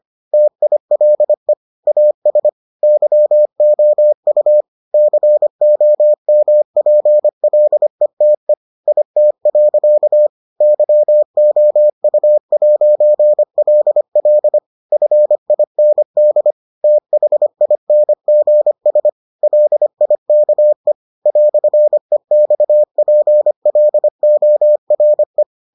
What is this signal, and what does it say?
Morse code, heard as a keyed tone, about 25 wpm, 600 Hz. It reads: TILE AS YOU COMPLETE IT. YOU'LL FIND THINGS LIKE "EXPLORE